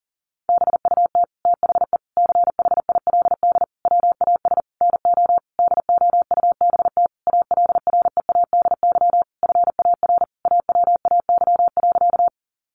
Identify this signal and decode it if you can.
Morse code, heard as a keyed tone, about 40 wpm, 700 Hz. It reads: BUT THE CHILD WAS NO DOUBT ALREADY FAR AWAY.
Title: Sentences from Top 1500 Words 40wpm